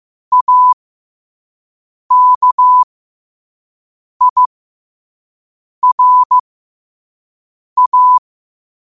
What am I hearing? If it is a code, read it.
Morse code, heard as a keyed tone, about 15 wpm, spaced as for 5 wpm, 1000 Hz. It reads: AKIRA
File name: qotdfiles / 5wpm